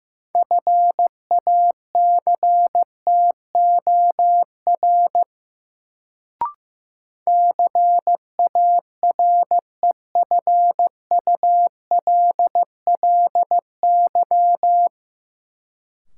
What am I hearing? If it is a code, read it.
Morse code, heard as a keyed tone, about 15 wpm, 700 Hz. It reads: FACTOR CAREFULLY
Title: Sentences from Top 1300 Words - New Word Drill - 15wpm